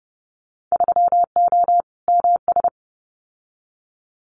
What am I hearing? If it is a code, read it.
Morse code, heard as a keyed tone, about 32 wpm, 700 Hz. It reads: 3O MS
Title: Winter Field Day Exchanges Encore 30wpm